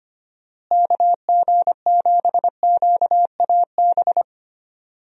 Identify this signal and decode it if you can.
Morse code, heard as a keyed tone, about 25 wpm, 700 Hz. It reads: KG7QAB
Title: USA Call Signs - 6 Characters - Repeated 3x - 25wpm